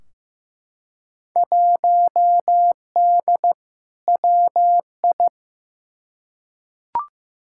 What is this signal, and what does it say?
Morse code, heard as a keyed tone, about 15 wpm, 700 Hz. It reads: 1D WI
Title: Field Day Exchanges - 15wpm